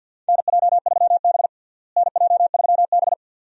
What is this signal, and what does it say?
Morse code, heard as a keyed tone, about 51 wpm, 700 Hz. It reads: NJ3B NJ3B
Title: USA Call Signs - 4 Characters - Repeated 2x - 50wpm